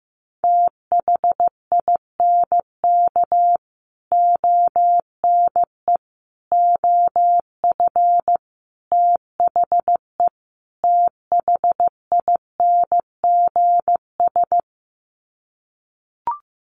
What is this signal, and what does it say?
Morse code, heard as a keyed tone, about 15 wpm, 700 Hz. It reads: THINK ONE OF THE THINGS
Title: Sets of 5 Words Encore2 15wpm